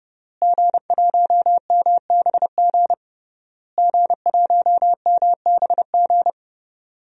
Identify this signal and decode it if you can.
Morse code, heard as a keyed tone, about 30 wpm, 700 Hz. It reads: G1MBG G1MBG
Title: DXCC-100 Call Signs - 5 Characters - Repeated 3x - 30wpm